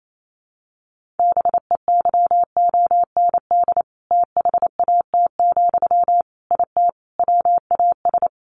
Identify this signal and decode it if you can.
Morse code, heard as a keyed tone, about 28 wpm, 700 Hz. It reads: BEYOND THAT, IT WAS
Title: Sentences from Top 1500 Words 28wpm